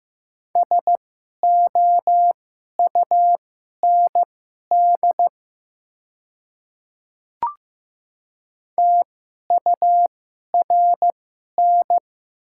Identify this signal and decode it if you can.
Morse code, heard as a keyed tone, about 15 wpm, spaced as for 11 wpm, 700 Hz. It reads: SOUND TURN
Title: 2x Character Spacing - Top 200 Words - 15wpm